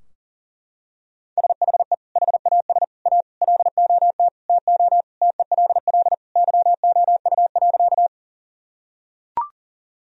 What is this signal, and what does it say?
Morse code, heard as a keyed tone, about 40 wpm, 700 Hz. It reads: SHE HAS A LOT TO TELL YOU.
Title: Sentences from Top 1000 Words - Review Part 3 - 40wpm